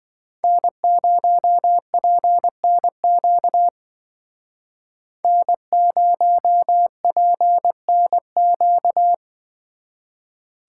Morse code, heard as a keyed tone, about 24 wpm, 700 Hz. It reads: N0PNQ N0PNQ
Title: Speed Racing - USA 1x3 Call Signs 20wpm